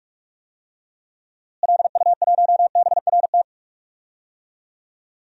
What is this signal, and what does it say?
Morse code, heard as a keyed tone, about 46 wpm, 700 Hz. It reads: LU1BRT
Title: DXCC-100 Call Signs - 6 Characters - Repeated 3x - 45wpm